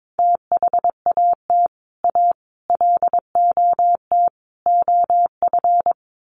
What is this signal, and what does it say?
Morse code, heard as a keyed tone, about 22 wpm, 700 Hz. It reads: THAT A LOT OF
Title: Sets of 4 Words 22wpm